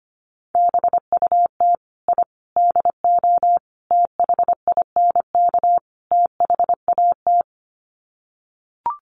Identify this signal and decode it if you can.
Morse code, heard as a keyed tone, about 25 wpm, 700 Hz. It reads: BUT I DO THINK THAT
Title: Sets of 5 Words Encore2 25wpm